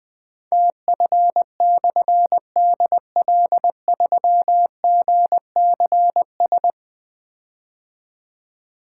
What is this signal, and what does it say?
Morse code, heard as a keyed tone, about 20 wpm, 700 Hz. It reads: TF/DL3GCS
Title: Difficult Call Signs 20wpm